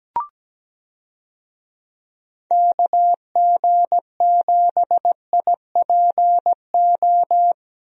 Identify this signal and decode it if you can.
Morse code, heard as a keyed tone, about 17 wpm, 700 Hz. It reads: KG7IPO
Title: USA Call Signs - 6 Characters - Repeated 2x - 17wpm